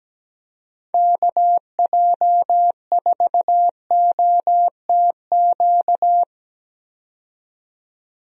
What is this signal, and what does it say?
Morse code, heard as a keyed tone, about 17 wpm, 700 Hz. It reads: KJ4OTQ
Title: USA Call Signs - 6 Characters - 17wpm